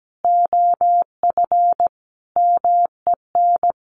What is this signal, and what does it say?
Morse code, heard as a keyed tone, about 17 wpm, 700 Hz. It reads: OF MEN
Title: Sentences from Top 1300 Words 17wpm